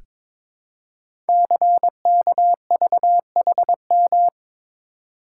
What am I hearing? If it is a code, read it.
Morse code, heard as a keyed tone, about 22 wpm, 700 Hz. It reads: CKVHM
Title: Code Groups - 5 Letters - 22wpm